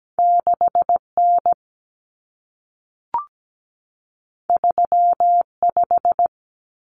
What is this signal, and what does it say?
Morse code, heard as a keyed tone, about 17 wpm, 700 Hz. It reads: 6N 35
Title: Common Cut Numbers - 2-Digit 17wpm